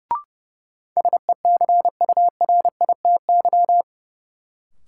Morse code, heard as a keyed tone, about 30 wpm, 700 Hz. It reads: SECURITY
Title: Sentences from Top 1000 Words - Review Part 4 - New Word Drill - 30wpm